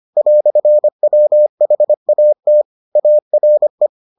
Morse code, heard as a keyed tone, about 25 wpm, 600 Hz. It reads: "WHAT ARE